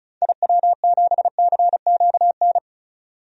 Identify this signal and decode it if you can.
Morse code, heard as a keyed tone, about 36 wpm, 700 Hz. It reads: IW7CQN
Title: DXCC-100 Call Signs - 6 Characters - 35wpm